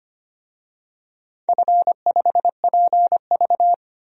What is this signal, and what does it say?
Morse code, heard as a keyed tone, about 25 wpm, 700 Hz. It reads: F5PV
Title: DXCC-100 Call Signs - 4 Characters - Repeated 2x - 25wpm